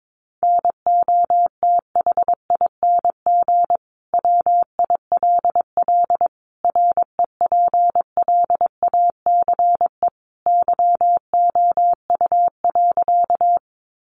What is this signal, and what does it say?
Morse code, heard as a keyed tone, about 22 wpm, 700 Hz. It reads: NOTHING WILL REPLACE YOU.
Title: Sentences from Top 1400 Words 22wpm